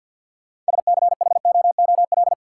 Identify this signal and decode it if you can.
Morse code, heard as a keyed tone, about 53 wpm, 700 Hz. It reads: SCHOOL
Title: Sentences from Top 300 Words - New Word Drill - 50wpm